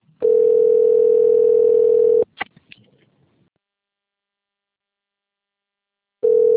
Ringback tone (North America), 440 + 480 Hz.